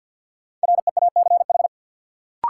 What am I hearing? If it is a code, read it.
Morse code, heard as a keyed tone, about 53 wpm, 700 Hz. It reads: REACH
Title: Sentences from Top 400 Words - New Word Drill - 50wpm